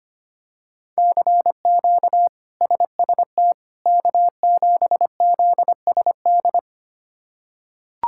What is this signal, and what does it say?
Morse code, heard as a keyed tone, about 25 wpm, 700 Hz. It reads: CQ SST K7ZSD
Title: K1USN SST QSO Practice - 25wpm